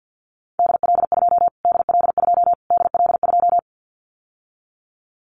Morse code, heard as a keyed tone, about 50 wpm, 700 Hz. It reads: DL2 DL2 DL2
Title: Callsign Prefixes for Top 25 DXCC - Repeated 3x - 50wpm